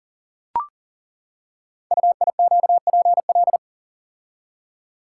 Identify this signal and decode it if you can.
Morse code, heard as a keyed tone, about 38 wpm, 690 Hz. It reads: UIQPL
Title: Code Groups - 5 Letters - 40wpm